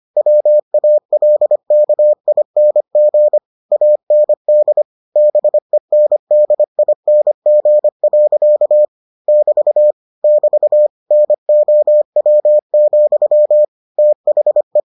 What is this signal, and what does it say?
Morse code, heard as a keyed tone, about 25 wpm, 600 Hz. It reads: WALKING AND BENDING. <BT> <BT> NOW, THE